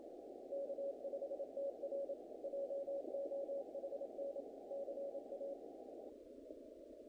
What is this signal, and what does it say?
Morse code, heard as a keyed tone, about 27 wpm, 560 Hz, with noise. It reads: K5TR JOHN CA
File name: contests / naqp-cw-k5tr-op-k5pi- -1900